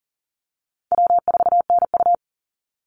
Morse code, heard as a keyed tone, about 37 wpm, 700 Hz. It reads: W4NU
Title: Call Signs - 40wpm